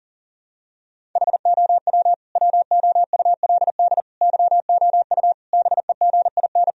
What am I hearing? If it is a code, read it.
Morse code, heard as a keyed tone, about 40 wpm, 700 Hz. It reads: HOW WOULD YOU BEGIN
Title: Sentences from Top 1000 Words - Review Part 1 - 40wpm